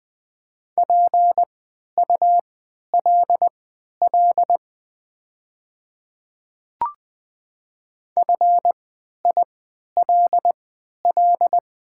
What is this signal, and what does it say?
Morse code, heard as a keyed tone, about 20 wpm, spaced as for 11 wpm, 700 Hz. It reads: PULL FILL
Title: 3x Character Spacing - Top 500 Words - 20wpm